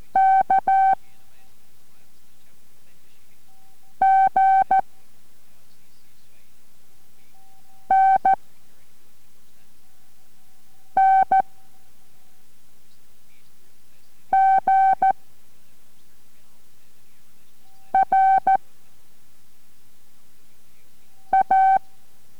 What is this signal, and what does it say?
Morse code, heard as a keyed tone, about 14 wpm, 770 Hz. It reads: K G N N G R A